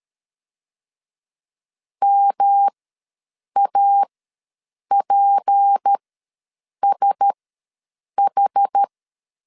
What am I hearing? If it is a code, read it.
Morse code, heard as a keyed tone, about 13 wpm, spaced as for 7 wpm, 800 Hz. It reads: MAPSH
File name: LEZIONI / 7 GF9Z / 7 GF ws 12wpm 3140